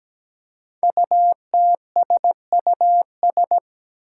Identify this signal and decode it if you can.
Morse code, heard as a keyed tone, about 17 wpm, 700 Hz. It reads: UTSUS
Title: Five Challenging Letter-Number Combinations 17wpm